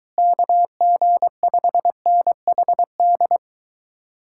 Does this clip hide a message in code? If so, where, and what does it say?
Yes: Morse code from 0.2 to 3.4 s, heard as a keyed tone, about 23 wpm, 700 Hz. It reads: KG5NHD